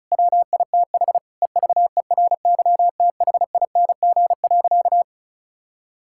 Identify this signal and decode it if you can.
Morse code, heard as a keyed tone, about 34 wpm, 700 Hz. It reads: WITH EVERYTHING.
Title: Sentences from Top 1000 Words - Review Part 5 - 35wpm